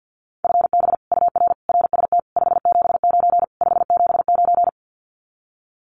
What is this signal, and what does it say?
Morse code, heard as a keyed tone, about 50 wpm, 700 Hz. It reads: FB UR RST 579 579
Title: Common QSO Phrases Encore 50wpm